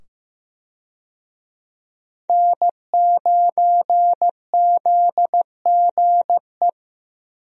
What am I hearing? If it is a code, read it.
Morse code, heard as a keyed tone, about 15 wpm, 700 Hz. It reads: N9ZGE
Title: USA Call Signs - 5 Characters - 15wpm